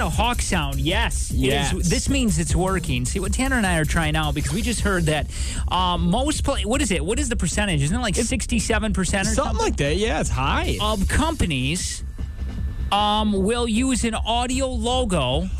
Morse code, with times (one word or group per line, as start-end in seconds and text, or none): none